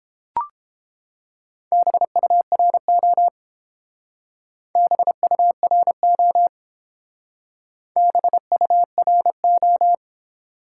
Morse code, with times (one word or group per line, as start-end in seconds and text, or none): 1.7-3.3 BURO
4.8-6.5 BURO
8.0-9.9 BURO